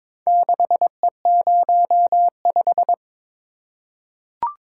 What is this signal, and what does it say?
Morse code, heard as a keyed tone, about 22 wpm, 700 Hz. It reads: 6E05
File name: practice / 22wpm / Four Letter-Number Combinations - 22wpm